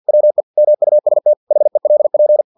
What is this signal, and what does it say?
Morse code, heard as a keyed tone, about 50 wpm, 600 Hz. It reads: WE MUST HELP
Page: Never 49wpm